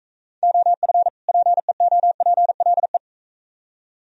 Morse code, heard as a keyed tone, about 39 wpm, 700 Hz. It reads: OF PEOPLE